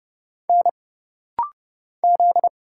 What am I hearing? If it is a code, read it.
Morse code, heard as a keyed tone, about 30 wpm, 700 Hz. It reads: N Z
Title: Single Letter-Number - Rapid-Fire 30wpm